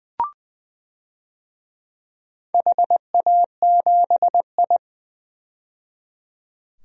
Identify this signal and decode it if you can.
Morse code, heard as a keyed tone, about 21 wpm, 700 Hz. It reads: HA7I
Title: DXCC-100 Call Signs - 4 Characters - 20wpm